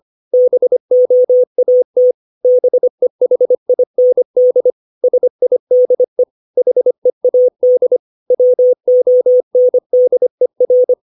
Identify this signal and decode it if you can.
Morse code, heard as a keyed tone, about 25 wpm, 500 Hz. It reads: BOAT BEHIND SIDE HEAD WONDER
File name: 632 words / 632 Words 25WPM